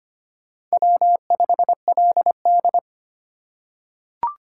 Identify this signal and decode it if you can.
Morse code, heard as a keyed tone, about 25 wpm, 700 Hz. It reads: W5LD
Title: USA Call Signs - 2x Character Spacing - 25wpm